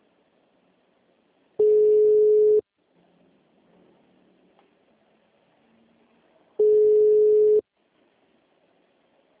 Ringback tone (Europe, 425 Hz).